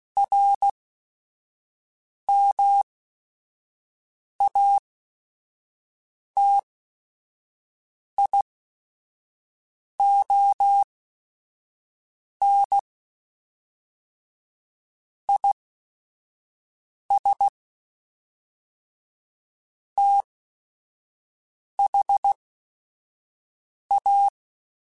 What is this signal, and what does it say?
Morse code, heard as a keyed tone, about 16 wpm, spaced as for 5 wpm, 800 Hz. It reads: RMATION IS THA
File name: Technical Subjects / Technical Subjects-mp3s / Sw Pwr Sup @5